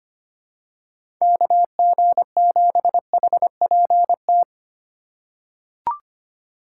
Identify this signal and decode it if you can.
Morse code, heard as a keyed tone, about 25 wpm, 700 Hz. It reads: KG7HPT
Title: USA Call Signs - 6 Characters - 3x Character Spacing - 25wpm